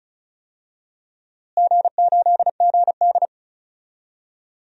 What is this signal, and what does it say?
Morse code, heard as a keyed tone, about 34 wpm, 700 Hz. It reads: G8GD
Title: DXCC-100 Call Signs - 4 Characters - Repeated 3x - 35wpm